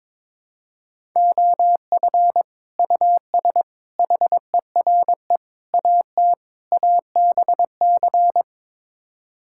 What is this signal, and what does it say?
Morse code, heard as a keyed tone, about 21 wpm, 700 Hz. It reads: OF US HERE AT ABC
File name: practice / 22wpm / Build A Phrase - 5-Word Phrases - 22wpm